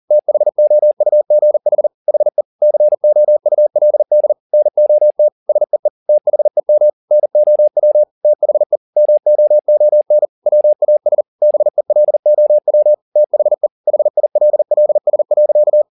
Morse code, heard as a keyed tone, about 40 wpm, 600 Hz. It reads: THOUGH HE COULD NOT SEE THEM NOW THE MOON WAS BELOW THE HILLS.